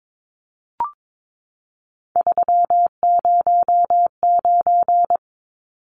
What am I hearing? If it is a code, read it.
Morse code, heard as a keyed tone, about 22 wpm, 700 Hz. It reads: 309
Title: Numbers - 3-Digit 22wpm